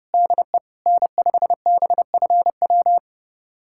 Morse code, heard as a keyed tone, about 32 wpm, 700 Hz. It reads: DE N5BFW
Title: USA Callsign DE Callsign - 5 Characters - 30wpm